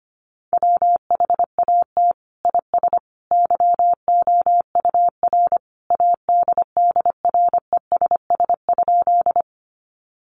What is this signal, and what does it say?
Morse code, heard as a keyed tone, about 25 wpm, 700 Hz. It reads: WHAT IS YOUR ADDRESS?
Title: Sentences from Top 1000 Words 25wpm